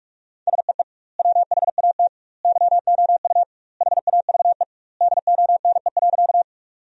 Morse code, heard as a keyed tone, about 46 wpm, 700 Hz. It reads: SEE WHAT YOU HAVE DONE.
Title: 2x Word-Spacing - Sentences from Top 500 Words - 45wpm